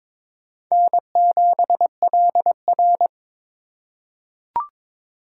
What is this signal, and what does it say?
Morse code, heard as a keyed tone, about 22 wpm, 700 Hz. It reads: N7LR
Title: K1USN SST QSO Practice - 22wpm